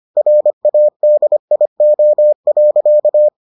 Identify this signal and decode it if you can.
Morse code, heard as a keyed tone, about 25 wpm, 600 Hz. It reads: RADIO.